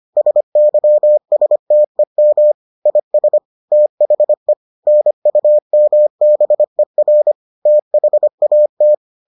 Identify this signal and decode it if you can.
Morse code, heard as a keyed tone, about 25 wpm, 600 Hz. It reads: SYSTEM IS THE NUMBER THAT